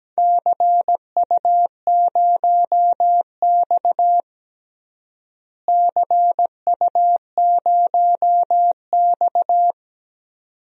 Morse code, heard as a keyed tone, about 17 wpm, 700 Hz. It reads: CU0X CU0X
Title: DXCC-100 Call Signs - 4 Characters - Repeated 2x - 17wpm